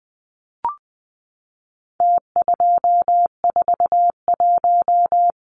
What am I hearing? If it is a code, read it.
Morse code, heard as a keyed tone, about 20 wpm, 700 Hz. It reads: T241
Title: Numbers - Contest Style - 4-Digit 20wpm